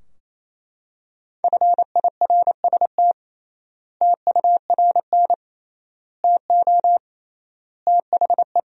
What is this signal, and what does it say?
Morse code, heard as a keyed tone, about 28 wpm, 700 Hz. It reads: FIRST TURN TO THE
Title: 3x Word-Spacing - Sentences from Top 300 Words - 28wpm